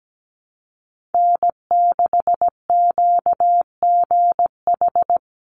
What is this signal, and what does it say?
Morse code, heard as a keyed tone, about 17 wpm, 700 Hz. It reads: N6QGH
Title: USA Call Signs 17wpm